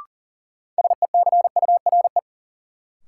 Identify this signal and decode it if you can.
Morse code, heard as a keyed tone, about 40 wpm, 700 Hz. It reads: SECURE
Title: Sentences from Top 1200 Words - New Word Drill - 40wpm